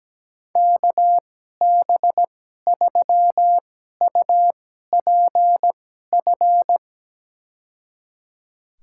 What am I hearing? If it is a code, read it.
Morse code, heard as a keyed tone, about 17 wpm, spaced as for 12 wpm, 700 Hz. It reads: KB3UPF
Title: USA Call Signs - 6 Characters - 2x Character Spacing - 17wpm